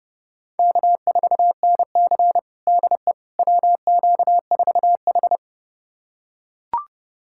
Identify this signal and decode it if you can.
Morse code, heard as a keyed tone, about 30 wpm, 700 Hz. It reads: K4NC DE WQ4H
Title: USA Callsign DE Callsign - 4 Characters - 30wpm